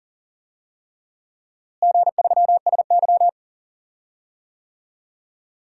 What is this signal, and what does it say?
Morse code, heard as a keyed tone, about 40 wpm, 700 Hz. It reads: G3SY
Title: DXCC-100 Call Signs - 4 Characters - Repeated 3x - 40wpm